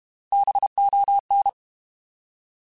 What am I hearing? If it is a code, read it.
Morse code, heard as a keyed tone, about 32 wpm, 800 Hz. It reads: BON
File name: Alphabetical / Alphabetical mp3s / Bs @30